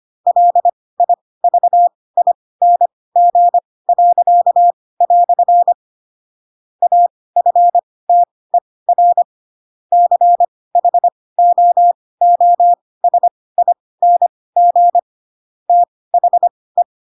Morse code, heard as a keyed tone, about 25 wpm, spaced as for 18 wpm, 700 Hz. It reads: LIVING." AFTER CHOOSING THE
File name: Pinocchio ch3 12 25 ik1hge